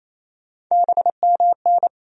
Morse code, heard as a keyed tone, about 28 wpm, 700 Hz. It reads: BMN
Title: USA Call Sign Postfixes - 3 Letters 28wpm